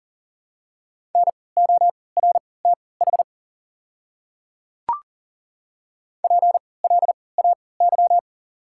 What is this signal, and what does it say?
Morse code, heard as a keyed tone, about 40 wpm, spaced as for 23 wpm, 690 Hz. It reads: NORTH PLAY
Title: 3x Character Spacing - Top 300 Words - 40wpm